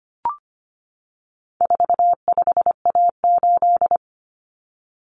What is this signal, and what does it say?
Morse code, heard as a keyed tone, about 25 wpm, 700 Hz. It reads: 45A8
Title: Common Cut Numbers - 4-Digit 25wpm